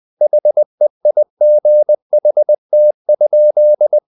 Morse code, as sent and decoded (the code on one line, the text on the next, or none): .... .. --. .... - ..--..
HEIGHT?